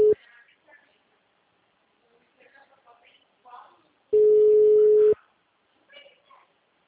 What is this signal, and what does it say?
Ringback tone (Europe, 425 Hz), with speech.
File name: out- -2007- - - .91259